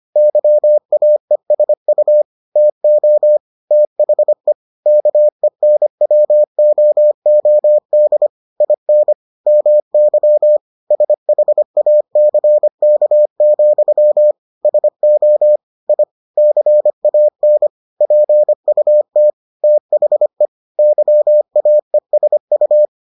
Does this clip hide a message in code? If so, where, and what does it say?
Yes: Morse code from 0.2 to 22.9 s, heard as a keyed tone, about 25 wpm, 600 Hz. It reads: YAESU TO THE KENWOOD IN MY SHACK, SO I CAN PUT THE YAESU